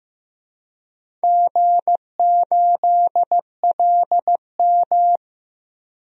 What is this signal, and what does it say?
Morse code, heard as a keyed tone, about 16 wpm, 700 Hz. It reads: G8LM
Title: DXCC-100 Call Signs - 4 Characters - 15wpm